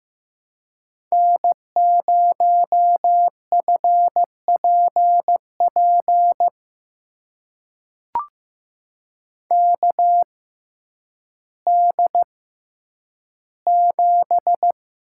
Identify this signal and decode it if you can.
Morse code, heard as a keyed tone, about 15 wpm, spaced as for 9 wpm, 700 Hz. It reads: N0FPP KD7
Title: USA Call Signs - 5 Characters - 6x Character Spacing - 15wpm